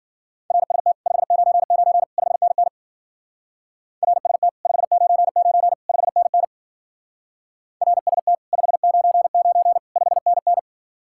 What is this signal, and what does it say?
Morse code, heard as a keyed tone, about 50 wpm, 700 Hz. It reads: RST 599 5NN RST 599 5NN RST 599 5NN
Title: Speed-Racing Common QSO Phrases Encore - 40wpm